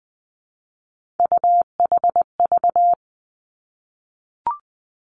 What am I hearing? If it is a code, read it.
Morse code, heard as a keyed tone, about 20 wpm, 700 Hz. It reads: UHV
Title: Three Challenging Letter-Number Combinations 20wpm